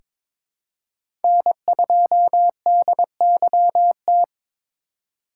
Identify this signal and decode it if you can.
Morse code, heard as a keyed tone, about 22 wpm, 700 Hz. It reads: N2DYT
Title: K1USN SST QSO Practice - 22wpm